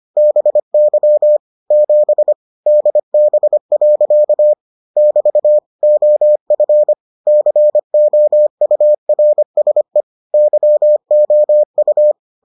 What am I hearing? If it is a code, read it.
Morse code, heard as a keyed tone, about 25 wpm, 600 Hz. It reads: BY 7 DB. <BT> OF COURSE YOU